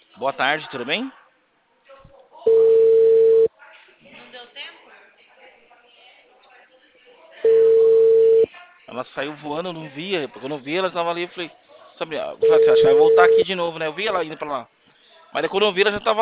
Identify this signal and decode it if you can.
Ringback tone (China, 450 Hz), with speech.
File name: out- -2016